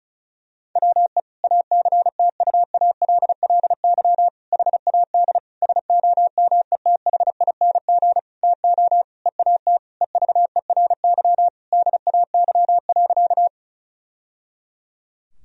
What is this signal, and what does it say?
Morse code, heard as a keyed tone, about 34 wpm, 700 Hz. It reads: WE ACTUALLY HAD SOMETHING TO EAT EVERY DAY.
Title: Sentences from Top 1000 Words - Review Part 4 - 35wpm